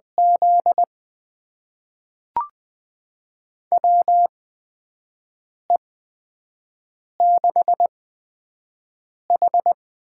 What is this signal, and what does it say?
Morse code, heard as a keyed tone, about 20 wpm, spaced as for 5 wpm, 700 Hz. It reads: Z WE6H
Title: USA Call Signs - 4 Characters - 8x Character Spacing - 20wpm